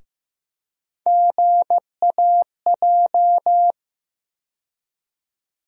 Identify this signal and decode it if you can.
Morse code, heard as a keyed tone, about 15 wpm, 700 Hz. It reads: GAJ
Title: USA Call Sign Postfixes - 3 Letters 15wpm